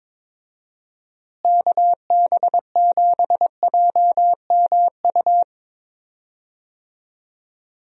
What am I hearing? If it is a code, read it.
Morse code, heard as a keyed tone, about 22 wpm, 700 Hz. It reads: KB7JMU
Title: USA Call Signs - 6 Characters - Repeated 3x - 22wpm